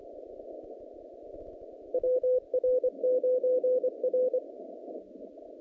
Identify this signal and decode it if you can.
Morse code, heard as a keyed tone, about 25 wpm, 510 Hz. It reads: WR9R
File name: contests / naqp-cw-k5tr-op-k5pi- -0000